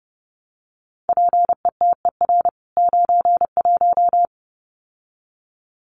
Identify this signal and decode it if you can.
Morse code, heard as a keyed tone, about 30 wpm, 700 Hz. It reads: PETER 91
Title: CWT Exchange 30wpm